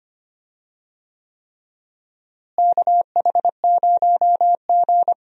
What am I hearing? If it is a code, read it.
Morse code, heard as a keyed tone, about 25 wpm, 700 Hz. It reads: KH0G
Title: USA Call Signs - 4 Characters - Repeated 3x - 25wpm